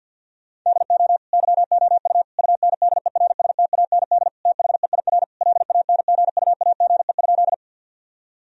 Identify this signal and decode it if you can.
Morse code, heard as a keyed tone, about 50 wpm, 700 Hz. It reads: DO YOU UNDERSTAND THEIR LANGUAGE?
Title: Sentences from Top 1000 Words - Review Part 3 - 50wpm